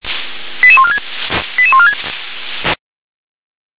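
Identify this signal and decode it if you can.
Five-tone selective call (ZVEI), heard as quick runs of five tones, tones of about 70 ms. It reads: 88146 88146
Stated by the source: ZVEI five-tone selective call 88740 received by SDR.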